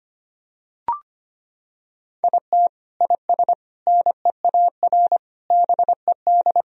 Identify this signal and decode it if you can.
Morse code, heard as a keyed tone, about 25 wpm, 700 Hz. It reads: IT IS NEAR BED
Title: Sentences from Top 1000 Words - Review Part 1 - 25wpm